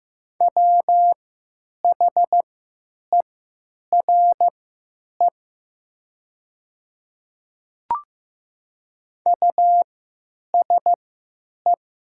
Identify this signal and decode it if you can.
Morse code, heard as a keyed tone, about 15 wpm, spaced as for 9 wpm, 700 Hz. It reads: WHERE USE
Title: 3x Character Spacing - Top 300 Words - 15wpm